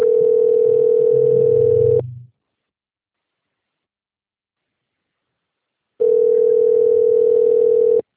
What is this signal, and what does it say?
Ringback tone (North America), 440 + 480 Hz.